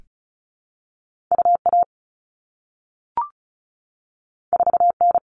Morse code, heard as a keyed tone, about 36 wpm, 700 Hz. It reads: UA 4N
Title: All Cut Numbers - 2-Digit 35wpm